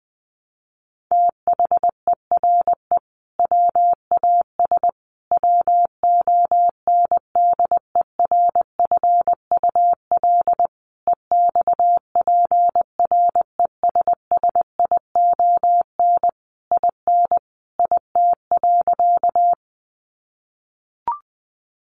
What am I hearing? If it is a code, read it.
Morse code, heard as a keyed tone, about 20 wpm, 700 Hz. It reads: THERE WAS WONDERFUL EXPRESSION IN IT.
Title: Sentences from Top 1200 Words 20wpm